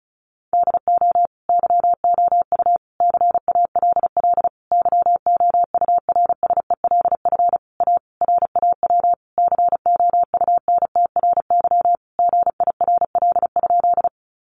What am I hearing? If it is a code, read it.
Morse code, heard as a keyed tone, about 34 wpm, 700 Hz. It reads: DO YOU CALL YOURSELF A RAW COUNTRY GIRL?
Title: Sentences from Top 1500 Words 35wpm